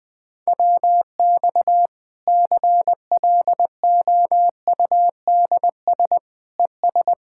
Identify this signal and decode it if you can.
Morse code, heard as a keyed tone, about 20 wpm, 700 Hz. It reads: WX CLOUDS ES
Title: Practice QSO - Exchange 2 - Info - 20wpm